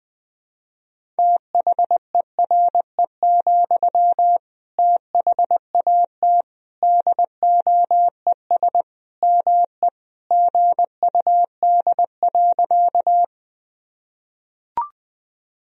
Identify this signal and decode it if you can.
Morse code, heard as a keyed tone, about 20 wpm, 700 Hz. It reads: THERE, THAT DOES ME GUD.
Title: Sentences from Top 200 Words and Abbreviations 20wpm